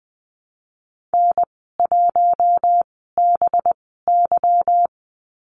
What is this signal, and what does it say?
Morse code, heard as a keyed tone, about 20 wpm, 700 Hz. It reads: N 1 B Y
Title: 200 Lesson 32g - Letter k - Call Signs 20wpm